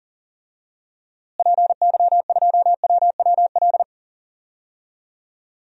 Morse code, heard as a keyed tone, about 40 wpm, 700 Hz. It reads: PY2WWL
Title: DXCC-100 Call Signs - 6 Characters - Repeated 2x - 40wpm